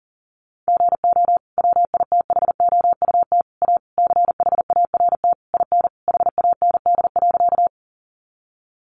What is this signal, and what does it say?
Morse code, heard as a keyed tone, about 40 wpm, 700 Hz. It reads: GO WITHOUT A CHART IN HAND.
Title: Sentences from Top 1300 Words 40wpm